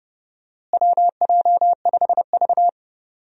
Morse code, heard as a keyed tone, about 30 wpm, 700 Hz. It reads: WJ5V
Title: USA Call Signs - 4 Characters - Repeated 3x - 30wpm